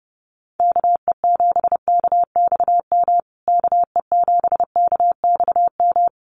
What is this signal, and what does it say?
Morse code, heard as a keyed tone, about 28 wpm, 700 Hz. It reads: KE7KXM KE7KXM
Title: USA Call Signs Repeated 2x 30wpm